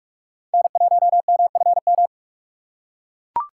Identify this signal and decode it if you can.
Morse code, heard as a keyed tone, about 39 wpm, 700 Hz. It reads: N1MFM